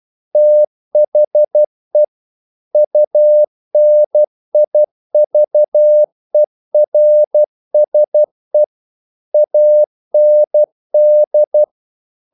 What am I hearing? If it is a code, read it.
Morse code, heard as a keyed tone, about 12 wpm, 600 Hz. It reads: THE UNIVERSE AND
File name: CW / Practice / EnglishQuotations0112